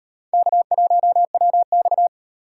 Morse code, heard as a keyed tone, about 38 wpm, 700 Hz. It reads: K1WX